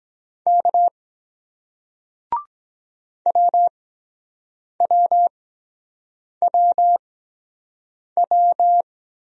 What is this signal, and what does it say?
Morse code, heard as a keyed tone, about 24 wpm, spaced as for 7 wpm, 700 Hz. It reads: K WWWW